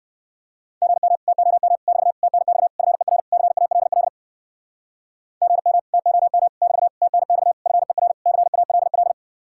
Morse code, heard as a keyed tone, about 59 wpm, 700 Hz. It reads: GM TOM <BT> TNX FER CALL GM TOM <BT> TNX FER CALL